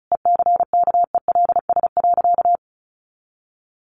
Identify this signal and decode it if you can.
Morse code, heard as a keyed tone, about 34 wpm, 700 Hz. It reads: ECKELS.
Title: A Sound Of Thunder 35wpm